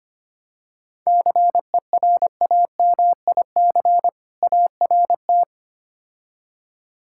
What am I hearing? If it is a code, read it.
Morse code, heard as a keyed tone, about 28 wpm, 700 Hz. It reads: CERAMIC ART
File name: practice / 25wpm / Difficult Words - Speed-Racing with Hints - 25wpm